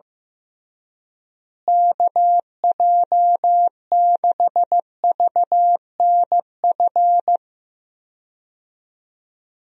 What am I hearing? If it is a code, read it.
Morse code, heard as a keyed tone, about 15 wpm, 700 Hz. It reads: KJ6VNF
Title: USA Call Signs - 6 Characters - Repeated 3x - 15wpm